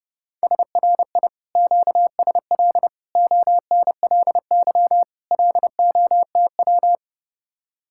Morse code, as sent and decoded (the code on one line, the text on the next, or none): ... .-. .. / --.- ... .-.. / --- -. .-.. -.-- / .-.. --- - .--
SRI QSL ONLY LOTW